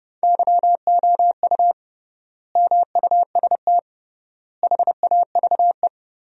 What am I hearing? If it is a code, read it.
Morse code, heard as a keyed tone, about 30 wpm, 700 Hz. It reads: YOU MUST HAVE
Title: 3x Word-Spacing - Sentences from Top 500 Words - 30wpm